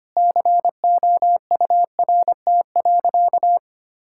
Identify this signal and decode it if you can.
Morse code, heard as a keyed tone, about 25 wpm, 700 Hz. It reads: COURT.
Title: Sentences from Top 1600 Words 25wpm